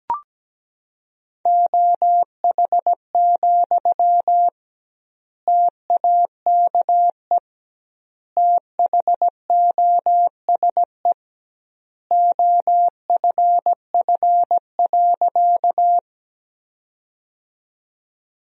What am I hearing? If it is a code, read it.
Morse code, heard as a keyed tone, about 17 wpm, 700 Hz. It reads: OH, TAKE THOSE OFF.
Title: 2x Word-Spacing - Sentences from Top 500 Words - 17wpm